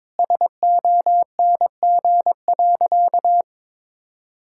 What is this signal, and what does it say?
Morse code, heard as a keyed tone, about 22 wpm, 700 Hz. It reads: SONG.
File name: practice / 22wpm / Sentences with 2-Word Phrases - 22wpm